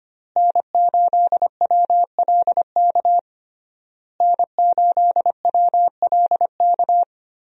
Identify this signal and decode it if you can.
Morse code, heard as a keyed tone, about 25 wpm, 700 Hz. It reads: N8WLK N8WLK
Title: USA Call Signs - 5 Characters - Repeated 3x - 25wpm